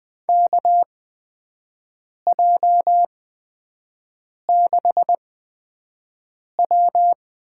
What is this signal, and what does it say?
Morse code, heard as a keyed tone, about 20 wpm, spaced as for 5 wpm, 700 Hz. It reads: KJ6W
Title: USA Call Signs - 6 Characters - 8x Character Spacing - 20wpm